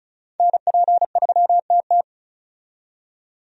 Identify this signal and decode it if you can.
Morse code, heard as a keyed tone, about 36 wpm, 700 Hz. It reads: NP3TT
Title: USA Call Signs - 5 Characters - 35wpm